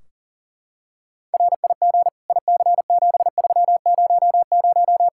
Morse code, heard as a keyed tone, about 40 wpm, 700 Hz. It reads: RIG IC7300
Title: Practice QSO - Exchange 2 - Info - 40wpm